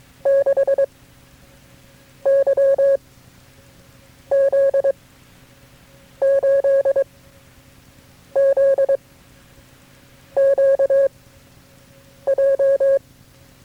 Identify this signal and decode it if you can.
Morse code, heard as a keyed tone, about 22 wpm, 550 Hz. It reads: 6 Y Z 8 Z Q J
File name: FreeFiles / CW-practice / KB6MT-tape3-side1